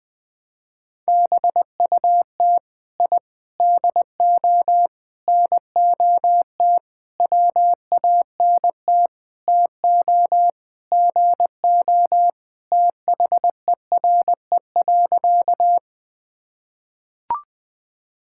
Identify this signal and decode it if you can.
Morse code, heard as a keyed tone, about 21 wpm, 700 Hz. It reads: BUT I DO NOT WANT TO GO THERE.
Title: Sentences with 5-Word Phrases - Spoken Beginning - 20wpm